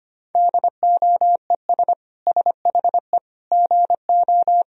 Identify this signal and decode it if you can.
Morse code, heard as a keyed tone, about 25 wpm, 700 Hz. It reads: DOES SHE GO
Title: Sentences from Top 1000 Words - Review Part 2 - 25wpm